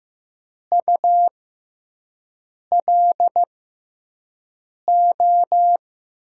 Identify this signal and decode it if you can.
Morse code, heard as a keyed tone, about 15 wpm, spaced as for 5 wpm, 700 Hz. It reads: ULO
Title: USA Call Signs - 5 Characters - 6x Character Spacing - 15wpm